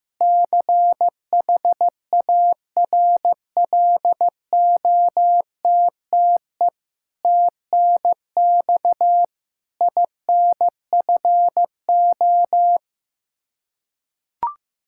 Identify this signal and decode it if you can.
Morse code, heard as a keyed tone, about 15 wpm, 700 Hz. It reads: CHARLOTTE TNX INFO
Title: Practice QSO - Exchange 3 - More Info - 15wpm